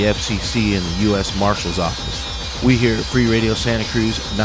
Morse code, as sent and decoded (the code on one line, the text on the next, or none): none